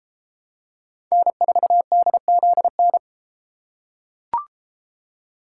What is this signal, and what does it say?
Morse code, heard as a keyed tone, about 33 wpm, 700 Hz. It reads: N4DZN